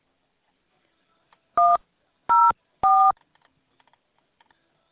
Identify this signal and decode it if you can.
DTMF, heard as telephone touch tones, tones of about 220 ms. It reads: 104